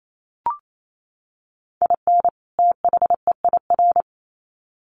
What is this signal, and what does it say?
Morse code, heard as a keyed tone, about 28 wpm, 700 Hz. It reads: IN THEIR
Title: Sets of 2 Words 28wpm